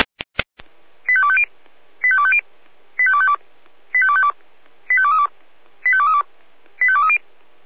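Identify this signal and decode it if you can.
Five-tone selective call (ZVEI), heard as quick runs of five tones, tones of about 70 ms. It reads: 85250 85250 85252 85252 85232 85232 85230
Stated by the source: ZVEI five-tone selective call received on SDR.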